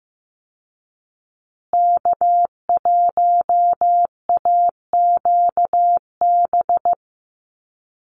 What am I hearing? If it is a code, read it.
Morse code, heard as a keyed tone, about 15 wpm, 700 Hz. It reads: K1AQB
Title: Call Signs Repeated 2x - 15wpm